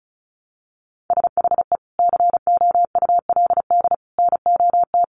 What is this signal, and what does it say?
Morse code, heard as a keyed tone, about 34 wpm, 700 Hz. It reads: SHE COULD NOT
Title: Sentences from Top 1400 Words 35wpm